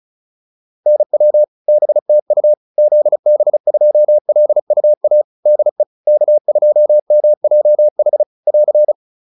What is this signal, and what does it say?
Morse code, heard as a keyed tone, about 34 wpm, 600 Hz. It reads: NW BTU ZB2LUA DE K2MJH <AR>
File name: QSO 402 35